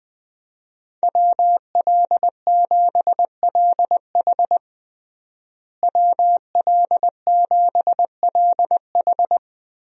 Morse code, heard as a keyed tone, about 20 wpm, 700 Hz. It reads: WL7LH WL7LH
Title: USA Call Signs - 5 Characters - Repeated 2x - 20wpm